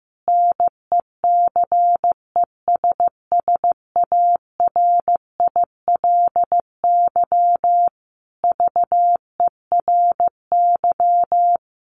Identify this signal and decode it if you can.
Morse code, heard as a keyed tone, about 15 wpm, 700 Hz. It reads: NECESSARILY VERY
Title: Sentences from Top 1400 Words 15wpm